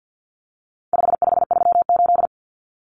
Morse code, heard as a keyed tone, about 50 wpm, 700 Hz. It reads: 5538
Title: Common Cut Numbers - 4-Digit 50wpm